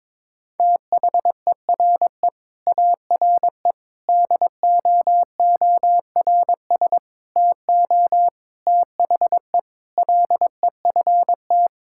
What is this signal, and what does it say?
Morse code, heard as a keyed tone, about 22 wpm, 700 Hz. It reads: THERE ARE DOORS TO THE LEFT
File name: practice / 22wpm / Sentences with 8-Word Phrases - 22wpm